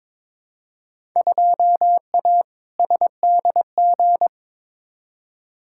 Morse code, heard as a keyed tone, about 21 wpm, 700 Hz. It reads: 2A SDG
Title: Field Day Exchanges Encore - 22wpm